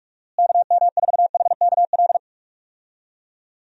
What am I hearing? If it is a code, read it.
Morse code, heard as a keyed tone, about 46 wpm, 700 Hz. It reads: KM4HKL
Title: USA Call Signs - 6 Characters - Repeated 2x - 45wpm